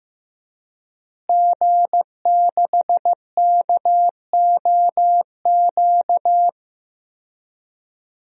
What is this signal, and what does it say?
Morse code, heard as a keyed tone, about 15 wpm, 700 Hz. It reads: G6KOQ
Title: DXCC-100 Call Signs - 5 Characters - Repeated 2x - 15wpm